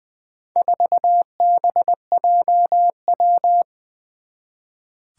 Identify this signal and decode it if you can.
Morse code, heard as a keyed tone, about 20 wpm, 700 Hz. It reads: 4BJW
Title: Four Challenging Letter-Number Combinations 20wpm